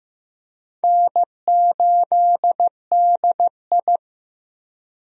Morse code, heard as a keyed tone, about 15 wpm, 700 Hz. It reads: N8DI
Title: USA Call Signs - 4 Characters - Repeated 3x - 15wpm